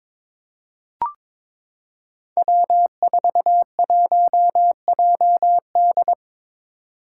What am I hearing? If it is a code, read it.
Morse code, heard as a keyed tone, about 22 wpm, 700 Hz. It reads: W41JD
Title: Five Challenging Letter-Number Combinations 22wpm